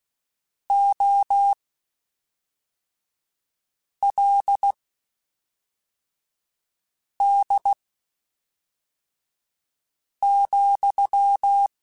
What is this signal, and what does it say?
Morse code, heard as a keyed tone, about 16 wpm, spaced as for 3 wpm, 800 Hz. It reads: OLD,
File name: (spaced) / QSOs Spaced-mp3s / kc7sac @5 sp